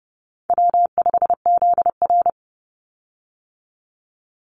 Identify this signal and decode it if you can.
Morse code, heard as a keyed tone, about 30 wpm, 700 Hz. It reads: W5ZR
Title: Easy Call Signs Repeated 2x - 30wpm